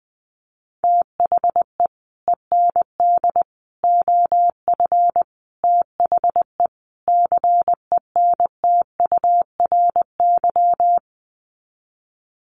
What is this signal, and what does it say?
Morse code, heard as a keyed tone, about 20 wpm, 700 Hz. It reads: THE END OF THE CENTURY
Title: Sets of 5 Words Encore2 20wpm